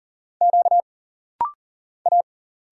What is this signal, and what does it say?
Morse code, heard as a keyed tone, about 40 wpm, 700 Hz. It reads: Q A
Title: Single Letters - Rapid-Fire 40wpm